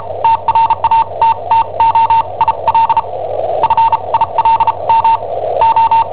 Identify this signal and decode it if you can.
Morse code, heard as a keyed tone, about 34 wpm, spaced as for 28 wpm, 910 Hz. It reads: TRATTO IL FILM O
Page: Cassetta esercizi preparata da I7OHP per IW5ELP